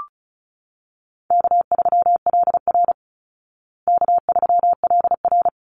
Morse code, heard as a keyed tone, about 34 wpm, 700 Hz. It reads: K3LR K3LR
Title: Call Signs Repeated 3x - 35wpm